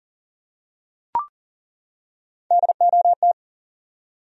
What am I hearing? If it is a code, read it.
Morse code, heard as a keyed tone, about 40 wpm, 700 Hz. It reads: DOT
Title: USA Common Names - Max 3 Letters - 40wpm